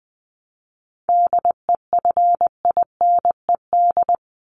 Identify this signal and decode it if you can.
Morse code, heard as a keyed tone, about 20 wpm, 700 Hz. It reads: DEFINED
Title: ED Suffix - Top 100 Words with Base Words 20wpm